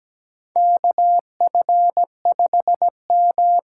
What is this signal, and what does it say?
Morse code, heard as a keyed tone, about 17 wpm, 700 Hz. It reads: KF5M
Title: USA Call Signs - 5 Characters - 8x Character Spacing - 17wpm